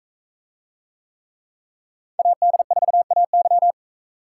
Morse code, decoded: AD4AY

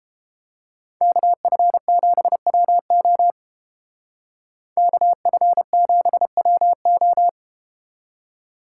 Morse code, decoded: KF7WO KF7WO

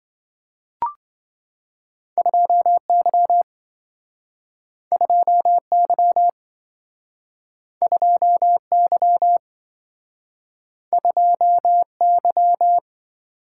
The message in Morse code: 2Y 2Y 2Y 2Y

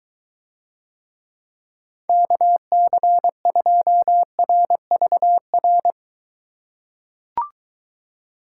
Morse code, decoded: KC2RVR

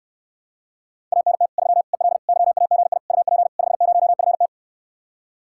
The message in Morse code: ANT 3 EL YAGI UP 50FT

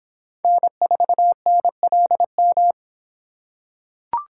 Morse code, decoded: N4NLM

